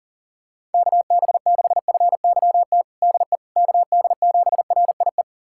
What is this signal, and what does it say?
Morse code, heard as a keyed tone, about 40 wpm, 700 Hz. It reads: KB6FYT DE KD7RIE
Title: USA Callsign DE Callsign - 6 Characters - 40wpm